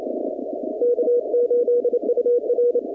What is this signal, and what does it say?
Morse code, heard as a keyed tone, about 28 wpm, 480 Hz. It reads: K8UR
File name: contests / 2019-cw-wpx-kt5j-k5pi- -2330 0-21